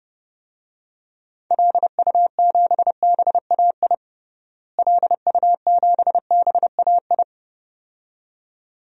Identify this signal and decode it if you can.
Morse code, heard as a keyed tone, about 32 wpm, 700 Hz. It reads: LU7BAI LU7BAI
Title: DXCC-100 Call Signs - 6 Characters - Repeated 2x - 30wpm